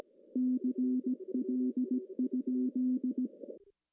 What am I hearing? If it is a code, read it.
Morse code, heard as a keyed tone, about 17 wpm, 250 Hz. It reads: CL?
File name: CW / cqww-cw-k5tr